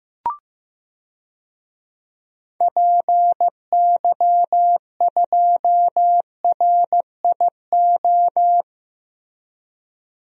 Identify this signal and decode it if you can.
Morse code, heard as a keyed tone, about 15 wpm, 700 Hz. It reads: PY2RIO
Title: DXCC-100 Call Signs - 6 Characters - Repeated 3x - 15wpm